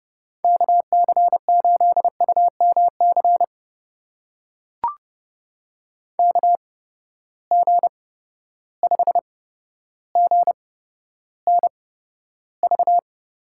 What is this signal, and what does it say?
Morse code, heard as a keyed tone, about 31 wpm, spaced as for 13 wpm, 700 Hz. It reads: KC8UMC KG5GNV
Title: USA Call Signs - 6 Characters - 8x Character Spacing - 30wpm